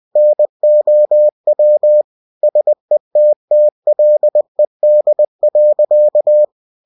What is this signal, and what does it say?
Morse code, heard as a keyed tone, about 20 wpm, 600 Hz. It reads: NOW SETTLED.
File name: cw-books / winnie-the-pooh / winnie-20wpm-page-0037